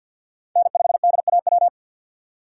Morse code, heard as a keyed tone, about 52 wpm, 700 Hz. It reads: N5DAW
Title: USA Call Signs - 5 Characters - 4x Character Spacing - 50wpm